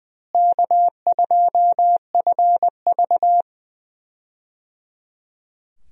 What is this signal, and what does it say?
Morse code, heard as a keyed tone, about 20 wpm, 700 Hz. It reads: K2FV
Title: USA Call Signs - 4 Characters - Repeated 3x - 20wpm